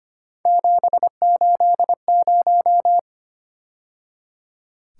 Morse code, heard as a keyed tone, about 32 wpm, 700 Hz. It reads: 780